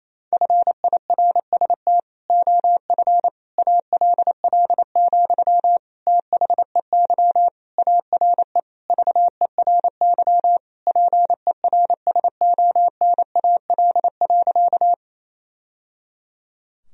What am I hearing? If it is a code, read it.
Morse code, heard as a keyed tone, about 28 wpm, 700 Hz. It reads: FIRST OF ALL, THEY ARE VERY PERSONAL.
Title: Sentences from Top 1000 Words - Review Part 4 - 28wpm